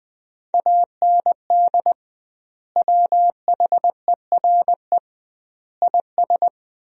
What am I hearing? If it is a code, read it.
Morse code, heard as a keyed tone, about 20 wpm, 700 Hz. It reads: AND WHERE IS
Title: 2x Word-Spacing - Sentences from Top 200 Words - 20wpm